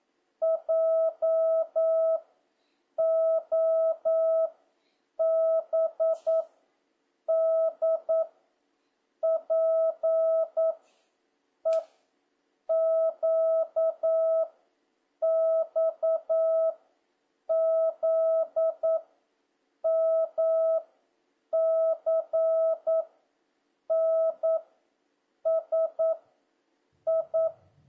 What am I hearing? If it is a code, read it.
Morse code, heard as a keyed tone, about 9 wpm, 650 Hz. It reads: J O B D P E Q X Z M C N S I